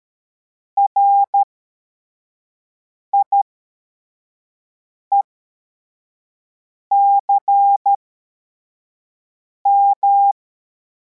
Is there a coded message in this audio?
Yes: Morse code from 0.8 to 10.3 s, heard as a keyed tone, about 13 wpm, spaced as for 4 wpm, 800 Hz. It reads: RIECM